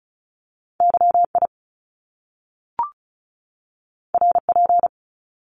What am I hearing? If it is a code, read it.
Morse code, heard as a keyed tone, about 36 wpm, 700 Hz. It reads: YI RP
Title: Top Two Letter Combinations in Words 35wpm